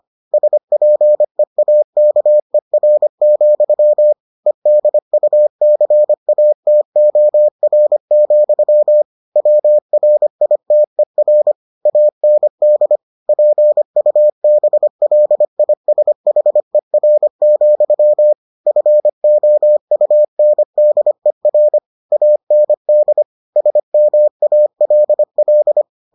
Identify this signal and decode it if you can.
Morse code, heard as a keyed tone, about 25 wpm, 600 Hz. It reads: SPEAKER, EDUCATOR, WRITER AND PUBLISHER, FOUNDER AND SMALL